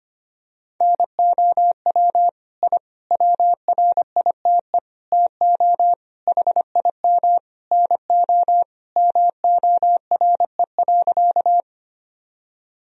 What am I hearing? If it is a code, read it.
Morse code, heard as a keyed tone, about 25 wpm, 700 Hz. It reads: NOW I WRITE TO HIM NO MORE.
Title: Sentences from Top 1000 Words - Review Part 1 - 25wpm